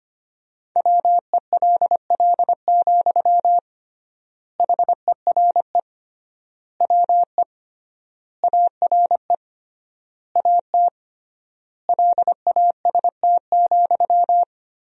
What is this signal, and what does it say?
Morse code, heard as a keyed tone, about 25 wpm, 700 Hz. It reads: WELL, HERE WE ARE AT LAST,
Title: 3x Word-Spacing - Sentences from Top 300 Words - 25wpm